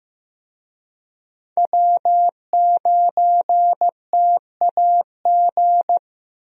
Morse code, heard as a keyed tone, about 15 wpm, 700 Hz. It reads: W9TAG
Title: USA Call Signs - 5 Characters - Repeated 3x - 15wpm